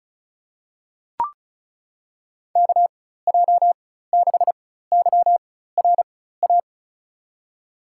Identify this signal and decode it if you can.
Morse code, heard as a keyed tone, about 34 wpm, spaced as for 16 wpm, 700 Hz. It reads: KJ6YRA
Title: USA Call Signs - 6 Characters - 4x Character Spacing - 35wpm